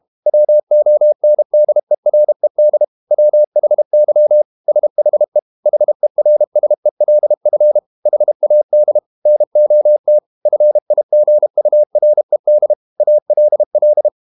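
Morse code, heard as a keyed tone, about 32 wpm, 600 Hz. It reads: WONDERED WHY SHE HERSELF HAD NOT FIGURED ALL
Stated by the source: Morse code practice at 32 wpm from CW med Gnister, reading Ken Follett's novel Never in English.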